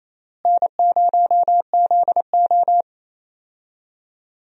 Morse code, heard as a keyed tone, about 28 wpm, 700 Hz. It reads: N0ZO
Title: USA Call Signs - 4 Characters - Repeated 3x - 28wpm